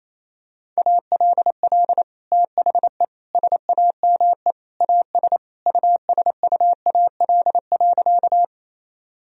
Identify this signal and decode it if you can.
Morse code, heard as a keyed tone, about 28 wpm, 700 Hz. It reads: ALL THE SAME AS USUAL.
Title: Sentences from Top 1000 Words - Review Part 2 - 28wpm